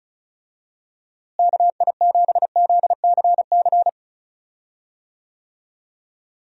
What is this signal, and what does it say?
Morse code, heard as a keyed tone, about 36 wpm, 700 Hz. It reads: KI7ZCC
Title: USA Call Signs - 6 Characters - Repeated 2x - 35wpm